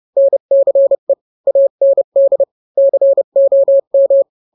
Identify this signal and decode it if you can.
Morse code, heard as a keyed tone, about 30 wpm, 550 Hz. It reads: NCE AND COM